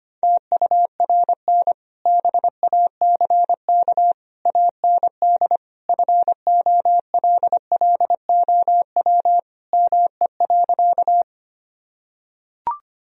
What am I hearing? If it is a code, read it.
Morse code, heard as a keyed tone, about 25 wpm, 700 Hz. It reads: TURN BACK AND FOLLOW ME.
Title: Sentences from Top 1000 Words - Review Part 1 - 25wpm